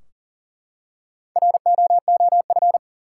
Morse code, heard as a keyed tone, about 37 wpm, 700 Hz. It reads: ROOF
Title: Four Letter Words - 40wpm